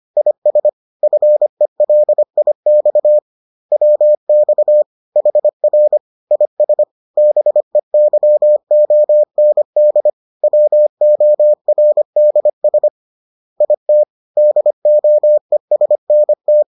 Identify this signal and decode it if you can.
Morse code, heard as a keyed tone, about 25 wpm, 600 Hz. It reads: IS FELIX WX HR IS BEYOND WORDS IT DOESNT
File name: 25wpm-6